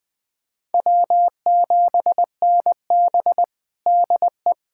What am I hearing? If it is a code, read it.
Morse code, heard as a keyed tone, about 20 wpm, 700 Hz. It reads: W7NB DE
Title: USA Callsign DE Callsign - 4 Characters - 20wpm